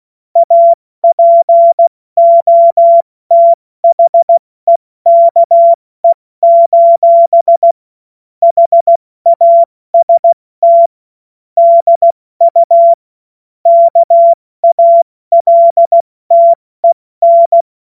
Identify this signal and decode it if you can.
Morse code, heard as a keyed tone, about 16 wpm, 680 Hz. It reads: APOTHEKE: HAST DU KALTEN
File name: Morsekurs / Witze / Witz 25. Mai 2021, 70 F80